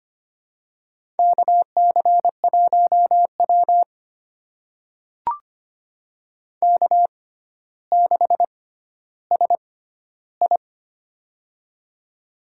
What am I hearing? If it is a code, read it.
Morse code, heard as a keyed tone, about 25 wpm, spaced as for 13 wpm, 700 Hz. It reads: KC1W K6SI
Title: USA Call Signs - 4 Characters - 6x Character Spacing - 25wpm